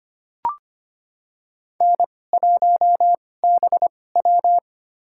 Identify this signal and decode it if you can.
Morse code, heard as a keyed tone, about 25 wpm, spaced as for 18 wpm, 700 Hz. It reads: N1BW
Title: USA Call Signs - 2x Character Spacing - 25wpm